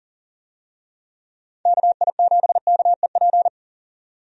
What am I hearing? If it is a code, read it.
Morse code, heard as a keyed tone, about 40 wpm, 700 Hz. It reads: KI7KEP
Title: USA Call Signs - 6 Characters - 40wpm